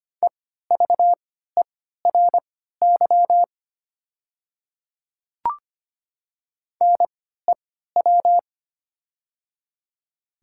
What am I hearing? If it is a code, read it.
Morse code, heard as a keyed tone, about 25 wpm, spaced as for 14 wpm, 690 Hz. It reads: EVERY NEW
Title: 3x Character Spacing - Top 200 Words - 25wpm